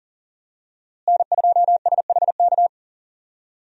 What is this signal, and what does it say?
Morse code, heard as a keyed tone, about 40 wpm, 700 Hz. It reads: N2SHK
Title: USA Call Signs - 5 Characters - Repeated 3x - 40wpm